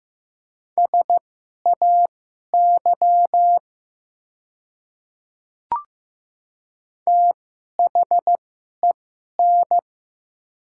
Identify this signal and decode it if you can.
Morse code, heard as a keyed tone, about 16 wpm, spaced as for 11 wpm, 700 Hz. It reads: SAY THEN
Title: 2x Character Spacing - Top 200 Words - 15wpm